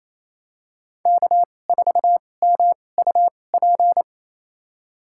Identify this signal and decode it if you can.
Morse code, heard as a keyed tone, about 28 wpm, spaced as for 20 wpm, 700 Hz. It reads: K4MUP
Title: USA Call Signs - 5 Characters - 2x Character Spacing - 28wpm